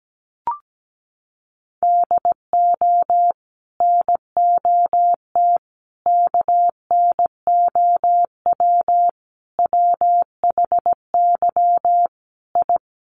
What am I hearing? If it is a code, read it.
Morse code, heard as a keyed tone, about 17 wpm, 700 Hz. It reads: DO NOT KNOW WHY I
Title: Sets of 5 Words Encore 17wpm